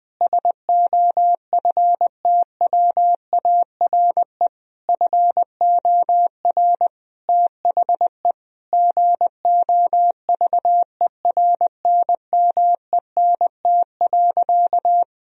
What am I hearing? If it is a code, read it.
Morse code, heard as a keyed tone, about 21 wpm, 700 Hz. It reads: SOFTWARE FOR THE GOVERNMENT.
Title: Sentences from Top 1000 Words - Review Part 3 - 20wpm